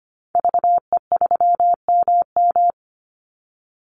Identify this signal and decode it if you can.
Morse code, heard as a keyed tone, about 25 wpm, 700 Hz. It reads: VE3MM
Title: Call Signs Repeated 2x - 25wpm